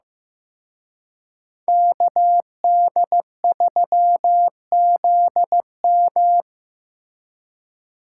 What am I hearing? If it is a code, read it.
Morse code, heard as a keyed tone, about 18 wpm, 700 Hz. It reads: KD3ZM
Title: Speed Racing - USA 2x2 Call Signs 15wpm